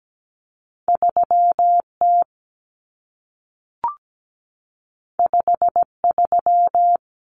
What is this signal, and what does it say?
Morse code, heard as a keyed tone, about 17 wpm, 700 Hz. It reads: 3T 53
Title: Common Cut Numbers - 2-Digit 17wpm